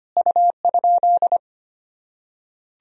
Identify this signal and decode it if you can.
Morse code, heard as a keyed tone, about 25 wpm, 700 Hz. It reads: U?
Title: Sentences from Top 200 Words and Abbreviations 25wpm